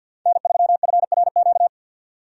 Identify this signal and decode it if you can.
Morse code, heard as a keyed tone, about 50 wpm, 700 Hz. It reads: N3FRQ
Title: USA Call Signs - 5 Characters - 50wpm